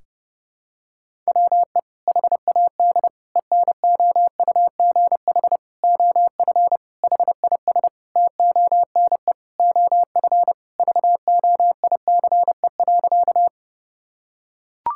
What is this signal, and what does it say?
Morse code, heard as a keyed tone, about 30 wpm, 700 Hz. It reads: WE HAD ENOUGH OF HIS TONE OF VOICE.
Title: Sentences from Top 1600 Words 30wpm